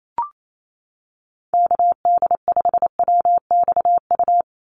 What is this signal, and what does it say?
Morse code, heard as a keyed tone, about 28 wpm, 700 Hz. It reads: KD5WXU
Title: USA Call Signs 28wpm